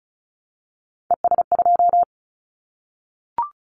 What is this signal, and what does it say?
Morse code, heard as a keyed tone, about 34 wpm, 700 Hz. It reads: ES2
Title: Callsign Prefixes for Top 100 DXCC - 35wpm